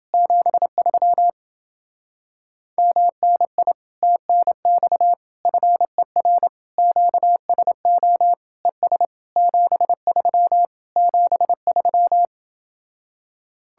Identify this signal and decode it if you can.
Morse code, heard as a keyed tone, about 29 wpm, 700 Hz. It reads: 73 MNI TNX FER QSO ES 73 73